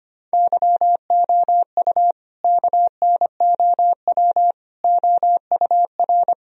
Morse code, heard as a keyed tone, about 25 wpm, 700 Hz. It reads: YOU KNOW OUR
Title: Sentences from Top 1000 Words - Review Part 4 - 25wpm